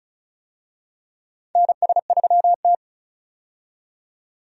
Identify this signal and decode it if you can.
Morse code, heard as a keyed tone, about 36 wpm, 700 Hz. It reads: NS3T
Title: USA Call Signs - 4 Characters - 35wpm